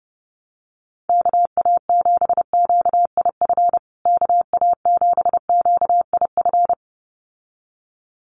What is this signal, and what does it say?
Morse code, heard as a keyed tone, about 30 wpm, 700 Hz. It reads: KA7QIF KA7QIF
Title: USA Call Signs Repeated 2x 30wpm